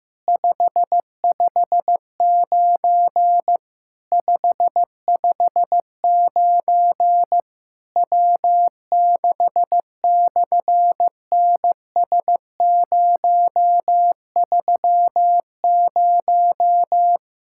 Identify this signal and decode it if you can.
Morse code, heard as a keyed tone, about 15 wpm, 700 Hz. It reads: 559 559 W6/NS030
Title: SOTA QSO Practice 15wpm